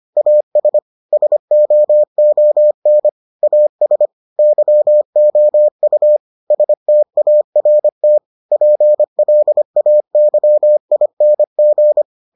Morse code, heard as a keyed tone, about 25 wpm, 600 Hz. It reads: AS SOON AS YOU START PLAYING